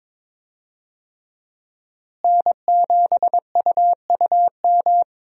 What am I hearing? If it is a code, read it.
Morse code, heard as a keyed tone, about 22 wpm, 700 Hz. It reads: N7UUM
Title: USA Call Signs - 5 Characters - Repeated 3x - 22wpm